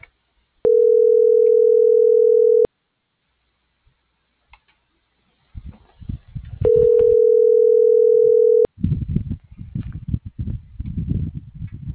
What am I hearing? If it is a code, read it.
Ringback tone (North America), 440 + 480 Hz.